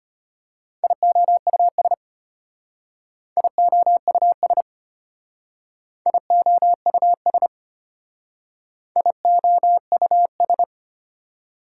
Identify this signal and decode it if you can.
Morse code, heard as a keyed tone, about 32 wpm, 700 Hz. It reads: IOUS IOUS IOUS IOUS